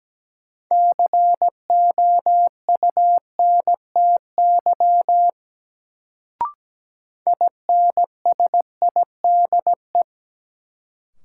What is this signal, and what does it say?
Morse code, heard as a keyed tone, about 17 wpm, 700 Hz. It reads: COUNTY INSIDE
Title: Sentences from Top 1200 Words - New Word Drill Alt - 17wpm